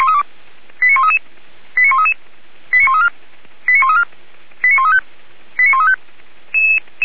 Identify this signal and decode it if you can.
Five-tone selective call (ZVEI), heard as quick runs of five tones, tones of about 70 ms. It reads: 89133 89133 89134 89134 89135 89135
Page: /464/SDR/ZVEI/89131/